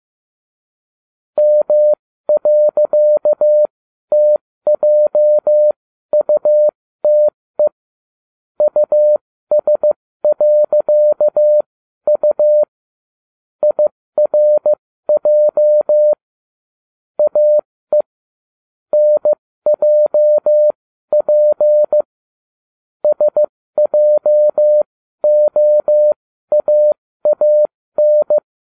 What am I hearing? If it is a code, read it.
Morse code, heard as a keyed tone, about 15 wpm, spaced as for 13 wpm, 600 Hz. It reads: M.TJUTE US.U IRJ AE NJP SJOAAN